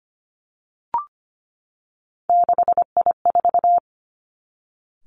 Morse code, heard as a keyed tone, about 25 wpm, 700 Hz. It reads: 6I4
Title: Three Challenging Letter-Number Combinations 25wpm